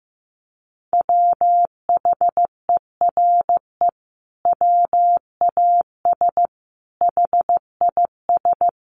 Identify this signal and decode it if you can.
Morse code, heard as a keyed tone, about 15 wpm, 700 Hz. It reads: WHERE WAS HIS
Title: Sentences from Top 1200 Words 15wpm